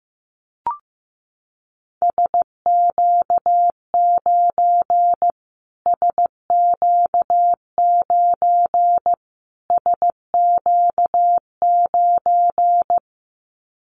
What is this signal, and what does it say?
Morse code, heard as a keyed tone, about 15 wpm, 700 Hz. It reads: SQ9 SQ9 SQ9
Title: Callsign Prefixes for Top 25 DXCC - Repeated 3x - 15wpm